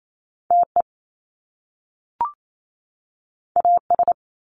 Morse code, heard as a keyed tone, about 28 wpm, 700 Hz. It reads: TE AS
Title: Top Two Letter Combinations in Words 28wpm